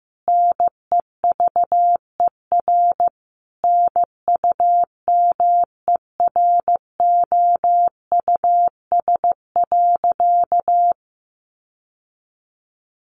Morse code, heard as a keyed tone, about 14 wpm, 700 Hz. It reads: NEVER NUMEROUS.
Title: Sentences from Top 1400 Words 15wpm